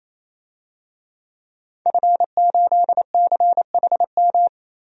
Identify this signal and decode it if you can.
Morse code, heard as a keyed tone, about 28 wpm, 700 Hz. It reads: F8CHM
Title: DXCC-100 Call Signs - 5 Characters - 28wpm